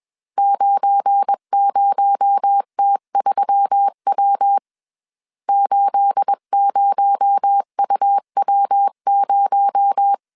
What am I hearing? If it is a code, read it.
Morse code, heard as a keyed tone, about 21 wpm, 800 Hz. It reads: 90T3W 80UW0